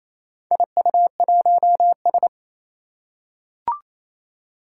Morse code, heard as a keyed tone, about 28 wpm, 700 Hz. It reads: IU1S
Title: Four Challenging Letter-Number Combinations 28wpm